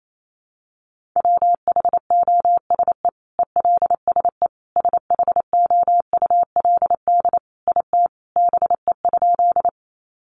Morse code, heard as a keyed tone, about 28 wpm, 700 Hz. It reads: WHOSE ELSE SHOULD IT BE?
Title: Sentences from Top 1200 Words 28wpm